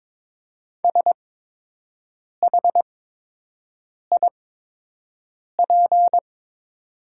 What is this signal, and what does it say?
Morse code, heard as a keyed tone, about 22 wpm, spaced as for 6 wpm, 700 Hz. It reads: SHIP
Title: 8x Character Spacing - Top 500 Words - 22wpm